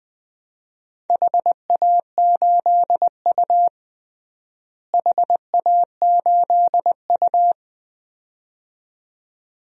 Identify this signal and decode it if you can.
Morse code, heard as a keyed tone, about 20 wpm, 700 Hz. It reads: HA8U HA8U
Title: DXCC-100 Call Signs - 4 Characters - Repeated 2x - 20wpm